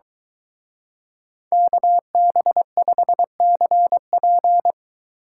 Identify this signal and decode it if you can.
Morse code, heard as a keyed tone, about 23 wpm, 700 Hz. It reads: KB5CP